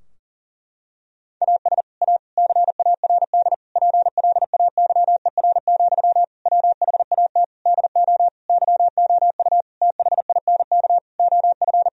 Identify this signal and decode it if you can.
Morse code, heard as a keyed tone, about 40 wpm, 700 Hz. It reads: AS A CARD PLAYER, WHAT DO YOU THINK OF
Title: Sentences from Top 1000 Words - Review Part 4 - 40wpm